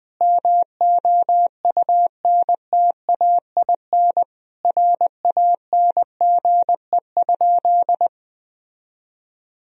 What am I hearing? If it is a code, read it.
Morse code, heard as a keyed tone, about 20 wpm, 700 Hz. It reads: MOUNTAIN RANGE?
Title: Sentences from Top 1000 Words - Review Part 4 - 20wpm